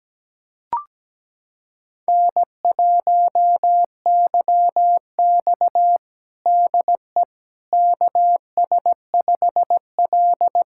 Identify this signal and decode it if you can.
Morse code, heard as a keyed tone, about 17 wpm, 700 Hz. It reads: N1YX DE KS5L
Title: USA Callsign DE Callsign - 4 Characters - 17wpm